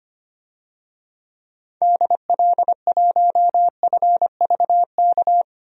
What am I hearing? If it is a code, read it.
Morse code, heard as a keyed tone, about 25 wpm, 700 Hz. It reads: DL1FVK
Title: DXCC-100 Call Signs - 6 Characters - 25wpm